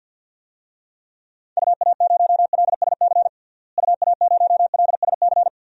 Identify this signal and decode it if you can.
Morse code, heard as a keyed tone, about 52 wpm, 700 Hz. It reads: UA0LSC UA0LSC
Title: DXCC-100 Call Signs - 6 Characters - Repeated 3x - 50wpm